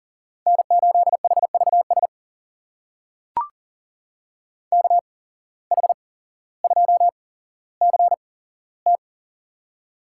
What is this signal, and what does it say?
Morse code, heard as a keyed tone, about 40 wpm, spaced as for 17 wpm, 700 Hz. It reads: N8HVS KH2CT
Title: USA Call Signs - 5 Characters - 8x Character Spacing - 40wpm